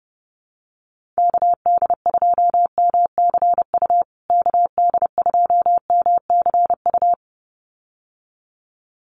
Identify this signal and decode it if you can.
Morse code, heard as a keyed tone, about 30 wpm, 700 Hz. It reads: KD2MCU KD2MCU
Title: USA Call Signs Repeated 2x 30wpm